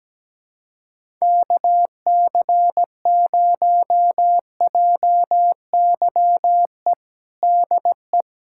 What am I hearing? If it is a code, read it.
Morse code, heard as a keyed tone, about 17 wpm, 700 Hz. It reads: KC0JYE DE
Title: USA Callsign DE Callsign - 6 Characters - 17wpm